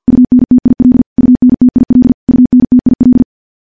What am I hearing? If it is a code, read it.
Morse code, heard as a keyed tone, about 50 wpm, 250 Hz. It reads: UNTIL UNTIL UNTIL
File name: morse / morse-training-50WPM